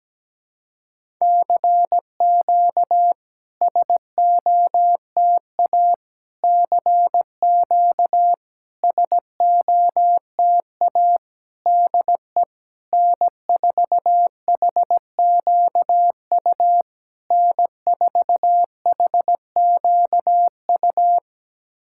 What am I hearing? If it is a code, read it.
Morse code, heard as a keyed tone, about 17 wpm, 700 Hz. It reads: CQ SOTA CQ SOTA DE N4HQU N4HQU
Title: SOTA QSO Practice 17wpm